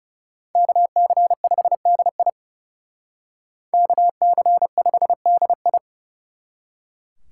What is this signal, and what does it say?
Morse code, heard as a keyed tone, about 38 wpm, 700 Hz. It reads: KC5DI KC5DI